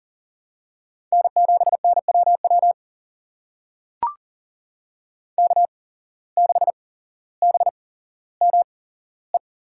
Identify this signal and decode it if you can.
Morse code, heard as a keyed tone, about 40 wpm, spaced as for 17 wpm, 700 Hz. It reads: N7NWW K6BME
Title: USA Call Signs - 5 Characters - 8x Character Spacing - 40wpm